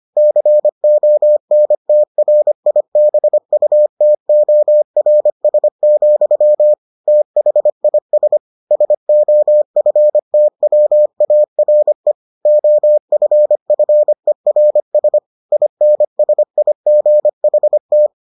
Morse code, heard as a keyed tone, about 25 wpm, 600 Hz. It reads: CONTRIBUTORS, THIS SOFTWARE OFFERS INSIGHT